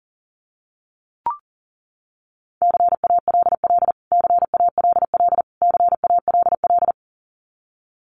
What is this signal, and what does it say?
Morse code, heard as a keyed tone, about 40 wpm, 700 Hz. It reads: CALL CALL CALL
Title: Top 200 Words Repeated 40wpm